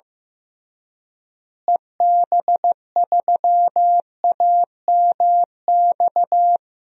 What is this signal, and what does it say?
Morse code, heard as a keyed tone, about 15 wpm, 700 Hz. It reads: EB3AMX
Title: DXCC-100 Call Signs - 6 Characters - Repeated 3x - 15wpm